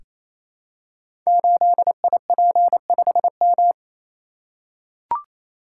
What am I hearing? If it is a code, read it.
Morse code, heard as a keyed tone, about 27 wpm, 700 Hz. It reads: 8IP5M
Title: Code Groups - 5 Letter:Numbers - 28wpm